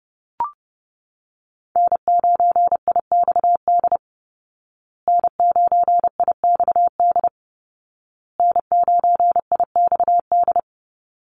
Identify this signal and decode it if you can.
Morse code, heard as a keyed tone, about 30 wpm, 700 Hz. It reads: N9IXD N9IXD N9IXD
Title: Call Signs Repeated 2x - 30wpm